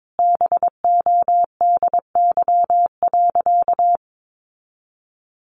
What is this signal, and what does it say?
Morse code, heard as a keyed tone, about 22 wpm, 700 Hz. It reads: BODY.
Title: A Sound Of Thunder 22wpm